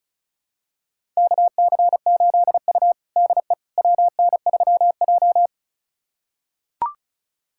Morse code, heard as a keyed tone, about 36 wpm, 700 Hz. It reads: KC8U DE WN3J
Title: USA Callsign DE Callsign - 4 Characters - 35wpm